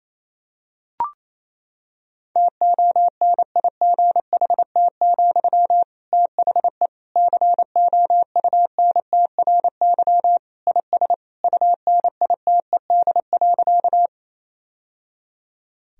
Morse code, heard as a keyed tone, about 28 wpm, 700 Hz. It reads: TONIGHT, THE COUNTRY IS UNITED.
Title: Sentences from Top 1000 Words - Review Part 4 - 28wpm